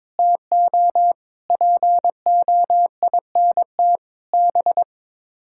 Morse code, heard as a keyed tone, about 22 wpm, 700 Hz. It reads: TO POINT B
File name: practice / 22wpm / Sets of 6 Words - 22wpm